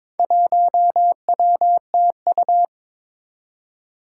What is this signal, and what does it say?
Morse code, heard as a keyed tone, about 22 wpm, 700 Hz. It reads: 1WTU
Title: Four Challenging Letter-Number Combinations 22wpm